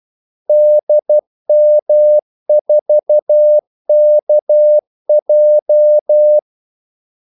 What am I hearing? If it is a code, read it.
Morse code, heard as a keyed tone, about 12 wpm, 600 Hz. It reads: DM4KJ
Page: 2 ekstra ordmellemrum mellem kaldesignalerne